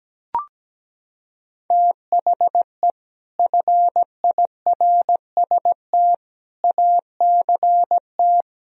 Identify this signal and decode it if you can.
Morse code, heard as a keyed tone, about 17 wpm, 700 Hz. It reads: THE FIRST ACT
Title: Sentences from Top 1000 Words - Review Part 1 - 17wpm